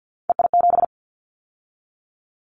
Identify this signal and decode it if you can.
Morse code, heard as a keyed tone, about 50 wpm, 700 Hz. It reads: EI7
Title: Callsign Prefixes for Top 50 DXCC - 50wpm